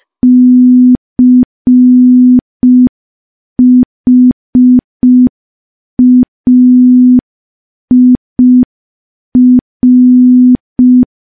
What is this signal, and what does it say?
Morse code, heard as a keyed tone, about 5 wpm, 250 Hz. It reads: CHAIR
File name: morse / morse-training-05WPM